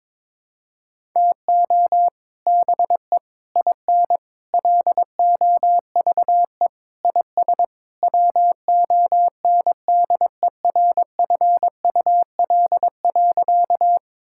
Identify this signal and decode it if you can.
Morse code, heard as a keyed tone, about 22 wpm, 700 Hz. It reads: TO BE IN LOVE IS WONDERFUL.
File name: practice / 22wpm / Sentences with 3-Word Phrases - 22wpm